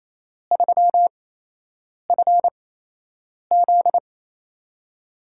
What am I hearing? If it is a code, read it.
Morse code, heard as a keyed tone, about 28 wpm, spaced as for 8 wpm, 700 Hz. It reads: 3FZ
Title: USA Call Signs - 4 Characters - 8x Character Spacing - 28wpm